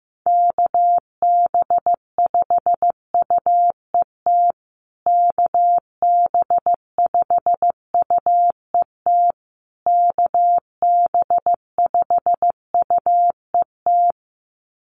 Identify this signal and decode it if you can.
Morse code, heard as a keyed tone, about 15 wpm, 700 Hz. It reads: KB5UET KB5UET KB5UET
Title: USA Call Signs Repeated 3x 15wpm